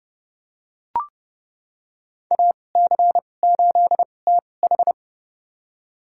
Morse code, heard as a keyed tone, about 30 wpm, spaced as for 22 wpm, 700 Hz. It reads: AC8TH
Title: USA Call Signs - 5 Characters - 2x Character Spacing - 30wpm